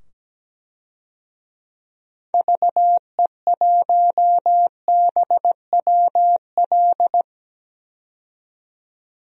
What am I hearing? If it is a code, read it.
Morse code, heard as a keyed tone, about 17 wpm, 700 Hz. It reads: VE1BWL
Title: DXCC-100 Call Signs - 6 Characters - 17wpm